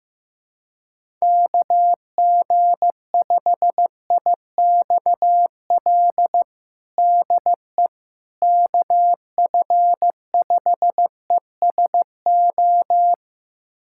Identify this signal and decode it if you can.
Morse code, heard as a keyed tone, about 15 wpm, 700 Hz. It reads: KG5IXL DE KF5ESO
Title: USA Callsign DE Callsign - 6 Characters - 15wpm